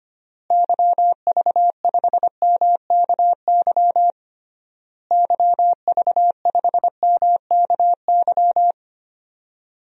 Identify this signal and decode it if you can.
Morse code, heard as a keyed tone, about 25 wpm, 700 Hz. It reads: YV5MKY YV5MKY
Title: DXCC-100 Call Signs - 6 Characters - Repeated 3x - 25wpm